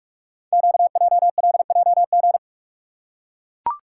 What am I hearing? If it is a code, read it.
Morse code, heard as a keyed tone, about 44 wpm, 700 Hz. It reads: QJLJG
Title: Code Groups - 5 Letters - 45wpm